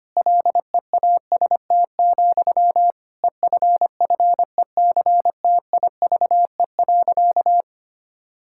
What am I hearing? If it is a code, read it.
Morse code, heard as a keyed tone, about 25 wpm, 700 Hz. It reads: LEAST, EFFECTIVE.
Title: Sentences from Top 1000 Words - Review Part 6 - 25wpm